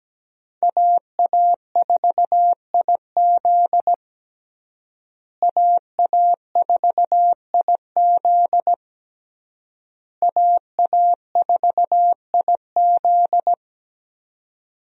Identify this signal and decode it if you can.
Morse code, heard as a keyed tone, about 17 wpm, 700 Hz. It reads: AA4IZ AA4IZ AA4IZ
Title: USA Call Signs - 5 Characters - Repeated 3x - 17wpm